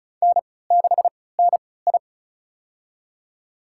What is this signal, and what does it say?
Morse code, heard as a keyed tone, about 34 wpm, spaced as for 20 wpm, 690 Hz. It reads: N6NI
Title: USA Call Signs - 3x Character Spacing - 35wpm